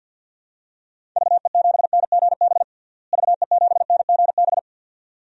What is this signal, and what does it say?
Morse code, heard as a keyed tone, about 47 wpm, 700 Hz. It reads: VE7NGB VE7NGB
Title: DXCC-100 Call Signs - 6 Characters - Repeated 2x - 50wpm